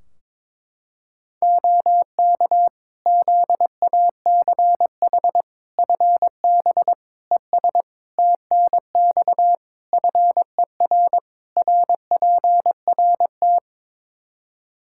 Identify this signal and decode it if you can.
Morse code, heard as a keyed tone, about 22 wpm, 700 Hz. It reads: OK ZACH FB ES TNX FER RPRT
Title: K1USN SST QSO Practice 22wpm